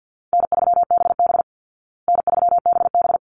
Morse code, heard as a keyed tone, about 50 wpm, 700 Hz. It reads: N3BB N3BB
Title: Easy Call Signs Repeated 2x - 50wpm